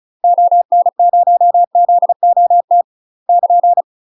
Morse code, heard as a keyed tone, about 36 wpm, 700 Hz. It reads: ON0ZOT <KN>
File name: UK MASTER QSOs TRAINING / 35WPM / UK MASTER QSOs 0024